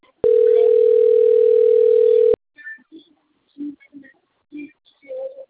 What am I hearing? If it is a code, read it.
Ringback tone (North America), 440 + 480 Hz.
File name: out- - - - - .65676